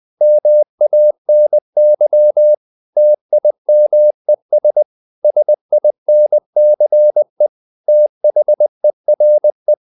Morse code, heard as a keyed tone, about 20 wpm, 600 Hz. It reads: MANY TIMES SINCE THERE
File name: the old man and the sea / the old man and the sea part 1 20wpm